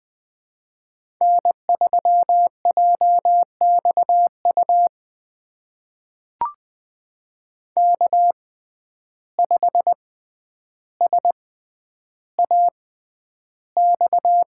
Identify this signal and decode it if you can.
Morse code, heard as a keyed tone, about 20 wpm, spaced as for 10 wpm, 700 Hz. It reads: N3JXU K5SAX
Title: USA Call Signs - 5 Characters - 6x Character Spacing - 20wpm